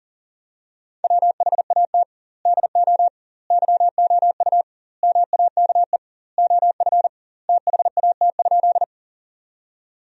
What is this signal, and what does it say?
Morse code, heard as a keyed tone, about 40 wpm, 700 Hz. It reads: WHAT DO YOU MAKE OF THAT?
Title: 2x Word-Spacing - Sentences from Top 100 Words - 40wpm